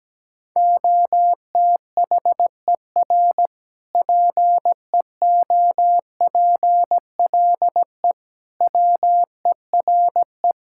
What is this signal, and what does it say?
Morse code, heard as a keyed tone, about 17 wpm, 700 Hz. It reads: OTHER PEOPLE WERE
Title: Sentences from Top 1000 Words - Review Part 5 - 17wpm